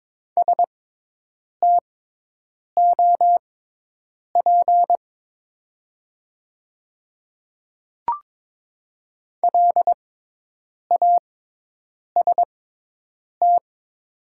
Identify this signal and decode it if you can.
Morse code, heard as a keyed tone, about 22 wpm, spaced as for 8 wpm, 700 Hz. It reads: STOP LAST
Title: 6x Character Spacing - Top 300 Words - 22wpm